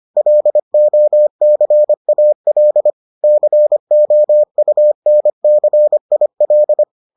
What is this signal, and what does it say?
Morse code, heard as a keyed tone, about 25 wpm, 600 Hz. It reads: LOCAL COUNCIL